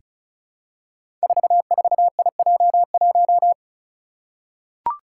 Morse code, heard as a keyed tone, about 36 wpm, 700 Hz. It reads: 44IJ1
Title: Five Challenging Letter-Number Combinations 35wpm